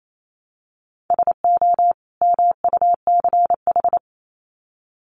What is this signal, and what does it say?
Morse code, heard as a keyed tone, about 28 wpm, 700 Hz. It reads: SO MUCH
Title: Sets of 2 Words 28wpm